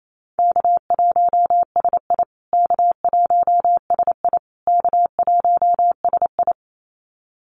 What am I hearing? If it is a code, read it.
Morse code, heard as a keyed tone, about 28 wpm, 700 Hz. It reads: K1SI K1SI K1SI
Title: USA Call Signs Repeated 3x 28wpm